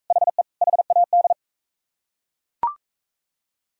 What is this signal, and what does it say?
Morse code, heard as a keyed tone, about 39 wpm, 700 Hz. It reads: HE HAD